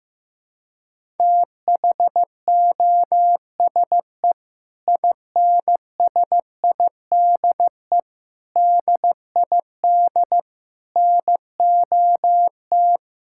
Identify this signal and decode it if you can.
Morse code, heard as a keyed tone, about 16 wpm, 700 Hz. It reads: THOSE INSIDE DID NOT
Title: Sentences from Top 1000 Words - Review Part 6 - 15wpm